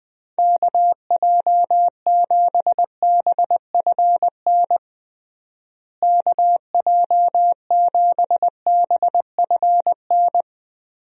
Morse code, heard as a keyed tone, about 21 wpm, 700 Hz. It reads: KJ7BFN KJ7BFN
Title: USA Call Signs - 6 Characters - Repeated 3x - 20wpm